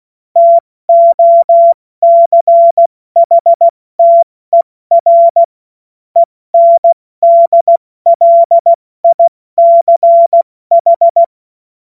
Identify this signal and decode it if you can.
Morse code, heard as a keyed tone, about 16 wpm, spaced as for 14 wpm, 680 Hz. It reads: TOCHTER ENDLICH
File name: Morsekurs / Witze / Witz 30. Mai 2023 70 F80